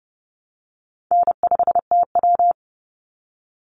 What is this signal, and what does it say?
Morse code, heard as a keyed tone, about 30 wpm, 700 Hz. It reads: N5TW
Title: Easy Call Signs Repeated 2x - 30wpm